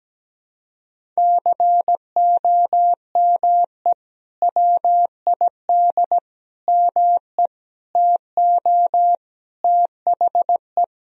Morse code, heard as a keyed tone, about 17 wpm, 700 Hz. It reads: COME WID ME TO THE
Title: Sentences from Top 200 Words and Abbreviations 17wpm